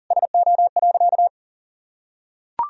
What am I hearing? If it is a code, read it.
Morse code, heard as a keyed tone, about 40 wpm, 700 Hz. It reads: SO.
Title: Sentences from Top 1000 Words - Review Part 2 - 40wpm